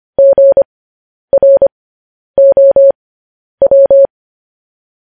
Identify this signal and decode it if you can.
Morse code, heard as a keyed tone, about 25 wpm, spaced as for 10 wpm, 550 Hz. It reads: GROW
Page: News Headlines in Morse Code at 10 WPM